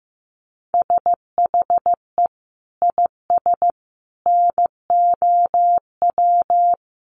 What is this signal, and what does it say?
Morse code, heard as a keyed tone, about 15 wpm, 700 Hz. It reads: SHE IS NOW
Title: Sentences from Top 900 Words 15wpm